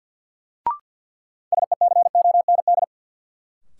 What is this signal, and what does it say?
Morse code, heard as a keyed tone, about 53 wpm, 700 Hz. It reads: SECOND
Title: Sentences from Top 300 Words - New Word Drill - 50wpm